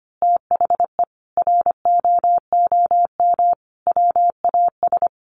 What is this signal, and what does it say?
Morse code, heard as a keyed tone, about 25 wpm, 700 Hz. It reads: THE ROOM WAS
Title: A Sound Of Thunder 25wpm